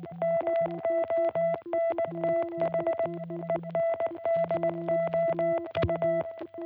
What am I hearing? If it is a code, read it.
Morse code, heard as a keyed tone, about 19 wpm, 660 Hz, with noise. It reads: MONTH END OU